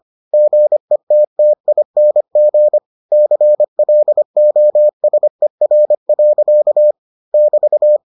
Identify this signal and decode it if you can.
Morse code, heard as a keyed tone, about 25 wpm, 600 Hz. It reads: GETTING CLOSER. <BT>